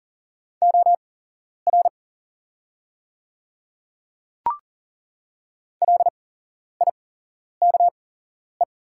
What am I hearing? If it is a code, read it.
Morse code, heard as a keyed tone, about 40 wpm, spaced as for 11 wpm, 690 Hz. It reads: OR LIKE
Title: 8x Character Spacing - Top 100 Words - 40wpm